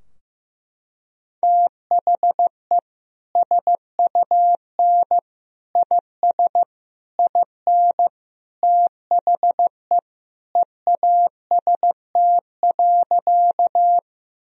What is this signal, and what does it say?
Morse code, heard as a keyed tone, about 15 wpm, 700 Hz. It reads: THE SUN IS IN THE EAST.
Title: Sentences from Top 1000 Words - Review Part 3 - 15wpm